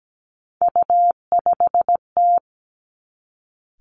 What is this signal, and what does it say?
Morse code, heard as a keyed tone, about 17 wpm, 700 Hz. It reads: U5T
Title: Three Challenging Letter-Number Combinations 17wpm